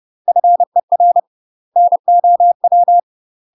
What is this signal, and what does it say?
Morse code, heard as a keyed tone, about 30 wpm, 700 Hz. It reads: FER NOW
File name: UK MASTER QSOs TRAINING / 30WPM / UK MASTER QSOs 0027